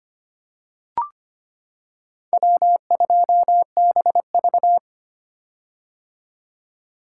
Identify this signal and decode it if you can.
Morse code, heard as a keyed tone, about 25 wpm, 700 Hz. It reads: W2BV
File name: practice / 25wpm / Four Letter-Number Combinations - 25wpm